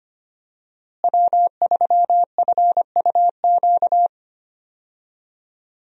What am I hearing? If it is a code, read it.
Morse code, heard as a keyed tone, about 25 wpm, 700 Hz. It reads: W3FUQ
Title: USA Call Signs - 5 Characters - Repeated 2x - 25wpm